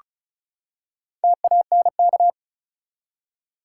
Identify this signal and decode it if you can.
Morse code, heard as a keyed tone, about 36 wpm, 700 Hz. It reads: TANK
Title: Four Letter Words - 35wpm